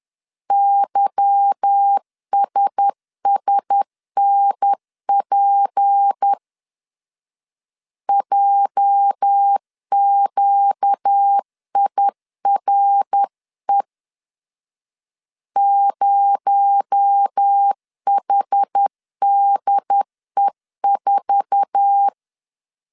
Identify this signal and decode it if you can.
Morse code, heard as a keyed tone, about 10 wpm, 800 Hz. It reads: YSSNP JQIRE 0HDE4